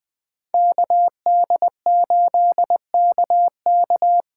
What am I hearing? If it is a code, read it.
Morse code, heard as a keyed tone, about 20 wpm, 700 Hz. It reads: KD8KK
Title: USA Call Signs - 8x Character Spacing - 20wpm